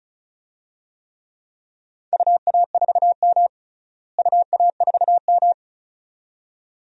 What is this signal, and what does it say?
Morse code, heard as a keyed tone, about 36 wpm, 700 Hz. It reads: UA4M UA4M
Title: DXCC-100 Call Signs - 4 Characters - Repeated 2x - 35wpm